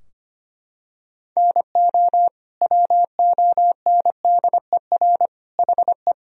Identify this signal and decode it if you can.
Morse code, heard as a keyed tone, about 25 wpm, 700 Hz. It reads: NO WONDER HE
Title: Sentences from Top 1000 Words - Review Part 3 - 25wpm